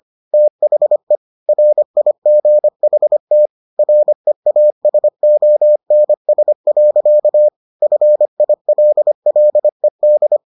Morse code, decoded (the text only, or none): THE RIGHT REASONS. FILLED